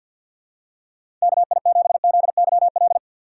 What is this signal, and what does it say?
Morse code, heard as a keyed tone, about 51 wpm, 700 Hz. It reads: KI7ZYL
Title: USA Call Signs - 6 Characters - 3x Character Spacing - 50wpm